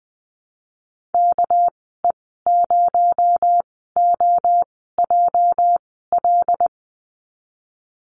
Morse code, heard as a keyed tone, about 20 wpm, spaced as for 15 wpm, 700 Hz. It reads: KE0OJL
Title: 245 Lesson 39g - Letter z - Call Signs 20wpm